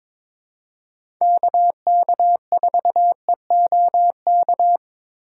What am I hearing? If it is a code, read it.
Morse code, heard as a keyed tone, about 22 wpm, 700 Hz. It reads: KK4EOK
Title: USA Call Signs - 6 Characters - 4x Character Spacing - 22wpm